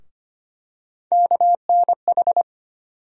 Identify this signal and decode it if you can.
Morse code, heard as a keyed tone, about 25 wpm, 700 Hz. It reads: KNH
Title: USA Call Sign Postfixes - 3 Letters 25wpm